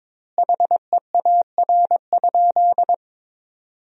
Morse code, heard as a keyed tone, about 22 wpm, 700 Hz. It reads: HEAR?
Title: 2x Word-Spacing - Sentences from Top 300 Words - 22wpm